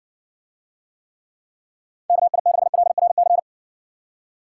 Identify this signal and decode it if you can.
Morse code, heard as a keyed tone, about 60 wpm, 700 Hz. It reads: KI6LRC